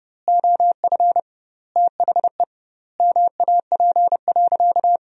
Morse code, heard as a keyed tone, about 30 wpm, 700 Hz. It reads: OF THE MAP.
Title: 2x Word-Spacing - Sentences from Top 400 Words - 30wpm